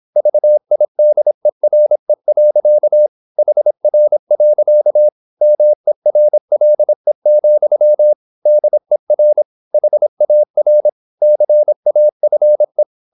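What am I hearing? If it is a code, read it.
Morse code, heard as a keyed tone, about 26 wpm, 600 Hz. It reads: VIDERE. HR. MERLE, DER HAR CAFE
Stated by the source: High-speed (QRQ) Morse at 26 wpm.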